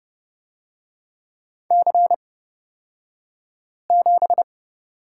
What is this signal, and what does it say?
Morse code, heard as a keyed tone, about 30 wpm, spaced as for 5 wpm, 700 Hz. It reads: C7
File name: Lesson 37d - Letter x - Two Characters 30wpm